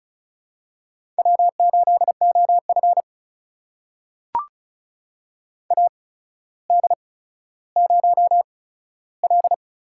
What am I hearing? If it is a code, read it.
Morse code, heard as a keyed tone, about 36 wpm, spaced as for 15 wpm, 700 Hz. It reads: W8OF AD0L
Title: USA Call Signs - 8x Character Spacing - 35wpm